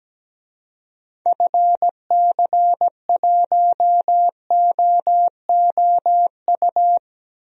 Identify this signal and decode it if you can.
Morse code, heard as a keyed tone, about 17 wpm, 700 Hz. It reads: FC1OOU
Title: DXCC-100 Call Signs - 6 Characters - Repeated 3x - 17wpm